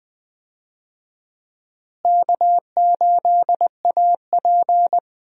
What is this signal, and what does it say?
Morse code, heard as a keyed tone, about 20 wpm, 700 Hz. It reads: K8AP
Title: USA Call Signs - 4 Characters - Repeated 2x - 20wpm